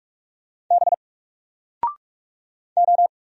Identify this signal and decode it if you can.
Morse code, heard as a keyed tone, about 44 wpm, 700 Hz. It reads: B O
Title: Single Letter-Number Alt 45wpm